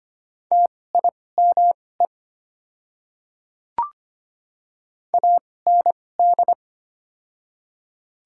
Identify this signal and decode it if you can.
Morse code, heard as a keyed tone, about 25 wpm, spaced as for 18 wpm, 700 Hz. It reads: TIME AND
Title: 2x Character Spacing - Top 100 Words - 25wpm